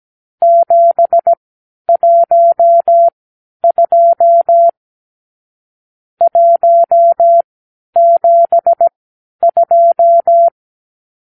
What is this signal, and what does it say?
Morse code, heard as a keyed tone, about 17 wpm, spaced as for 10 wpm, 680 Hz. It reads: 712 172
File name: k7qo / Lesson 52 (8)